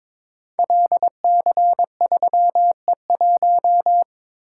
Morse code, heard as a keyed tone, about 22 wpm, 700 Hz. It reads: LC3E1
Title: Code Groups - 5 Letter:Numbers - 22wpm